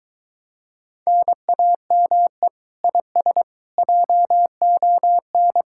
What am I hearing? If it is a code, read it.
Morse code, heard as a keyed tone, about 23 wpm, 700 Hz. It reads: NAME IS JON